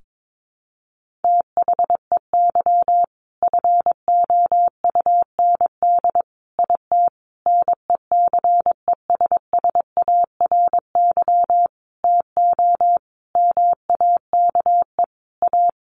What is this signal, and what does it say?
Morse code, heard as a keyed tone, about 22 wpm, 700 Hz. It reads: THEY FOUND IT NECESSARY TO MAKE A
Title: Sentences from Top 1400 Words 22wpm